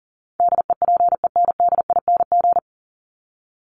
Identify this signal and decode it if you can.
Morse code, heard as a keyed tone, about 40 wpm, 700 Hz. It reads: DEPENDING
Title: ING Suffix - Top 100 Words with Base Words 40wpm